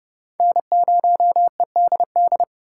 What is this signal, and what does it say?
Morse code, heard as a keyed tone, about 28 wpm, 700 Hz. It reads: N0EDD